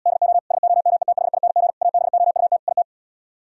Morse code, heard as a keyed tone, about 77 wpm, 700 Hz. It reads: DO SOMETHING ABOUT IT